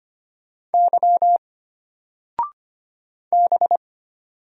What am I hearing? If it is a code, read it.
Morse code, heard as a keyed tone, about 25 wpm, 700 Hz. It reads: Y B
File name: Single-Letter-Number-25wpm